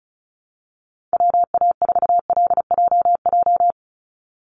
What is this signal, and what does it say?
Morse code, heard as a keyed tone, about 34 wpm, 700 Hz. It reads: WA4LJJ
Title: Call Signs - 35wpm